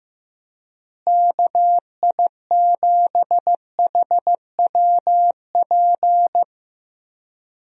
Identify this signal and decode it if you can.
Morse code, heard as a keyed tone, about 15 wpm, 700 Hz. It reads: KI7HWP
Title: USA Call Signs - 6 Characters - Repeated 3x - 15wpm